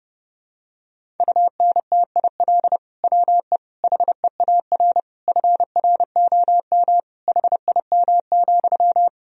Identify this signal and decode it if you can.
Morse code, heard as a keyed tone, about 30 wpm, 700 Hz. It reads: UNTIL WE HEAR FROM HIM,
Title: Sentences from Top 1000 Words - Review Part 1 - 30wpm